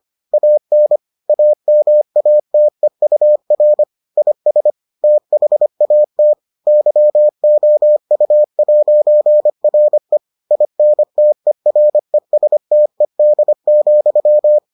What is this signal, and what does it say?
Morse code, heard as a keyed tone, about 25 wpm, 600 Hz. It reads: AN AMATEUR IS THAT YOU'RE INTERESTED,